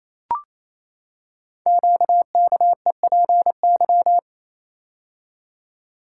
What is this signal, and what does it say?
Morse code, heard as a keyed tone, about 28 wpm, 700 Hz. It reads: QKEPY
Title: Code Groups - 5 Letters - 28wpm